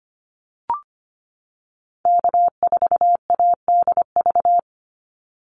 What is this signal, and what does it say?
Morse code, heard as a keyed tone, about 25 wpm, 700 Hz. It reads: K4ADV
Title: NA SOTA Activator Callsigns 25wpm